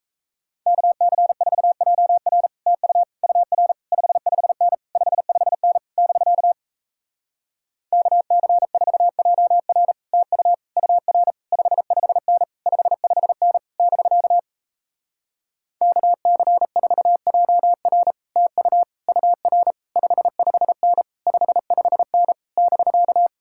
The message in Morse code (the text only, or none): KC4JR TU UR 55N 55N <BK> KC4JR TU UR 55N 55N <BK> KC4JR TU UR 55N 55N <BK>